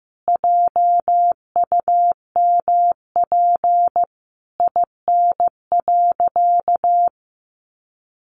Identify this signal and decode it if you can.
Morse code, heard as a keyed tone, about 15 wpm, 700 Hz. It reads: JUMP IN.
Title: Sentences from Top 1300 Words 15wpm